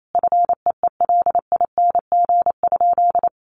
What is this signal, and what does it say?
Morse code, heard as a keyed tone, about 28 wpm, 700 Hz. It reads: FEELING?
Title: Sentences from Top 1200 Words 28wpm